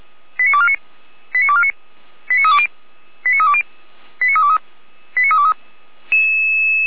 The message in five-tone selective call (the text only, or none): 89249 89249 89320 89320 89323 89323